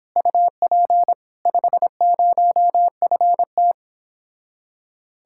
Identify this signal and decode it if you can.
Morse code, heard as a keyed tone, about 26 wpm, 700 Hz. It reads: UP 50FT
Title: Speed-Racing Common QSO Phrases - 22wpm